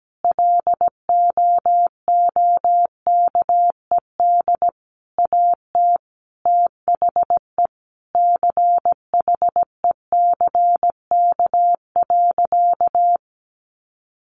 Morse code, heard as a keyed tone, about 17 wpm, 700 Hz. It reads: LOOKED AT THE CHECK.
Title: A Sound Of Thunder 17wpm